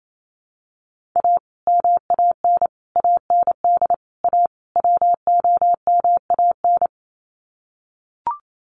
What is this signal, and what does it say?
Morse code, heard as a keyed tone, about 28 wpm, 700 Hz. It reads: A MAN AND A WOMAN
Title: Sets of 5 Words 28wpm